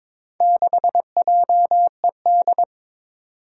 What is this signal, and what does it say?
Morse code, heard as a keyed tone, about 22 wpm, 700 Hz. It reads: 6JED
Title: Four Challenging Letter-Number Combinations 22wpm